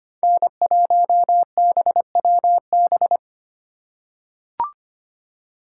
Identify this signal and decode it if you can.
Morse code, heard as a keyed tone, about 25 wpm, 700 Hz. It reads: N1BWB
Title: USA Call Signs - 5 Characters - 2x Character Spacing - 25wpm